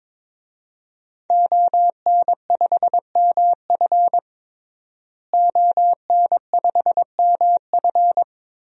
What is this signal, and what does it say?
Morse code, heard as a keyed tone, about 22 wpm, 700 Hz. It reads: ON5MF ON5MF
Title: DXCC-100 Call Signs - 5 Characters - Repeated 3x - 22wpm